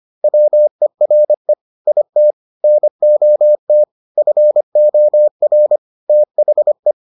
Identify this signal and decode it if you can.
Morse code, heard as a keyed tone, about 25 wpm, 600 Hz. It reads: WERE IT NOT FOR THE